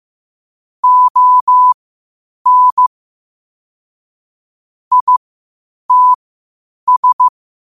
Morse code, heard as a keyed tone, about 15 wpm, spaced as for 8 wpm, 1000 Hz. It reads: ON ITS
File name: qotdfiles / 8wpm